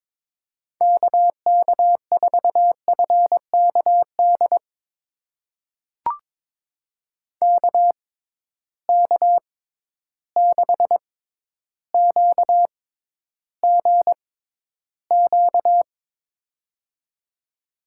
Morse code, heard as a keyed tone, about 22 wpm, spaced as for 11 wpm, 700 Hz. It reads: KK4FKD KK6QGQ
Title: USA Call Signs - 6 Characters - 6x Character Spacing - 22wpm